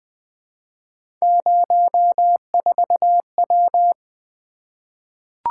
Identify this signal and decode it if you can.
Morse code, heard as a keyed tone, about 20 wpm, 700 Hz. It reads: 04W
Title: Three Letter-Number Combinations - 20wpm